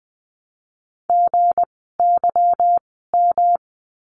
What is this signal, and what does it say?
Morse code, heard as a keyed tone, about 20 wpm, spaced as for 14 wpm, 700 Hz. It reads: GYM
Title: 160 Lesson 26f - Letter g - Words 20wpm